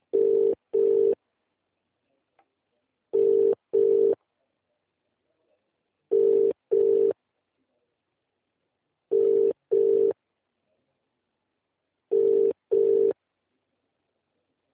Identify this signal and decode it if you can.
Ringback tone (UK), 400 + 450 Hz.